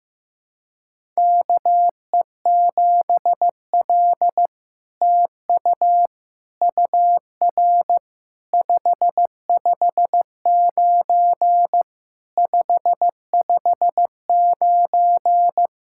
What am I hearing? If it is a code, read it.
Morse code, heard as a keyed tone, about 16 wpm, 700 Hz. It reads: KE7L TU UR 559 559
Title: SOTA QSO Practice 15wpm